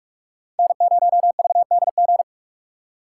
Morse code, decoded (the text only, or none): N0VDG